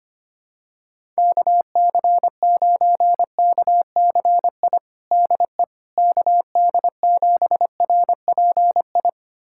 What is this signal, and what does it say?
Morse code, heard as a keyed tone, about 25 wpm, 700 Hz. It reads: KC9KCI DE KD7RPI
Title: USA Callsign DE Callsign 25wpm